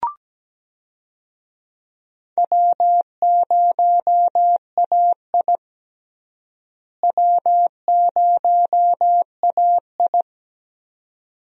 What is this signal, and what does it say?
Morse code, heard as a keyed tone, about 17 wpm, 700 Hz. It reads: W0AI W0AI
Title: USA Call Signs - 4 Characters - Repeated 2x - 17wpm